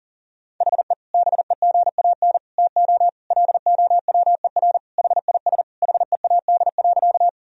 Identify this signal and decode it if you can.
Morse code, heard as a keyed tone, about 40 wpm, 700 Hz. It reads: HE BEGAN TO LOWER HIS HEAD.
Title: Sentences from Top 1000 Words - Review Part 4 - 40wpm